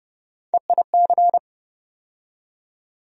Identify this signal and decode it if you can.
Morse code, heard as a keyed tone, about 32 wpm, 700 Hz. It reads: EIC
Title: USA Call Sign Postfixes - 3 Letters 30wpm